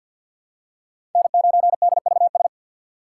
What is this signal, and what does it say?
Morse code, heard as a keyed tone, about 47 wpm, 700 Hz. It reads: N9DVS
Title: USA Call Signs - 5 Characters - 4x Character Spacing - 50wpm